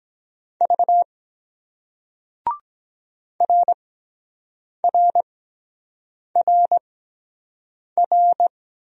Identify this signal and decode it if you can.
Morse code, heard as a keyed tone, about 23 wpm, spaced as for 7 wpm, 700 Hz. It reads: V RRRR